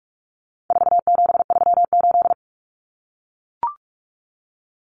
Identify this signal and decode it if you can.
Morse code, heard as a keyed tone, about 44 wpm, 700 Hz. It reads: 4738
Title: Numbers - 4-Digit 45wpm